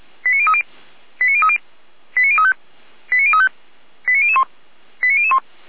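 Five-tone selective call (ZVEI), heard as quick runs of five tones, tones of about 70 ms. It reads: 89030 89030 89035 89035 89001 89001
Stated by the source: ZVEI five-tone selective call received via SDR, hosted by Feuerwehr Orlamünde.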